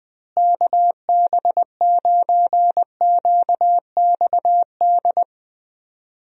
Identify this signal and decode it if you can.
Morse code, heard as a keyed tone, about 20 wpm, 700 Hz. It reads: KB9QXD
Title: USA Call Signs - 6 Characters - Repeated 3x - 20wpm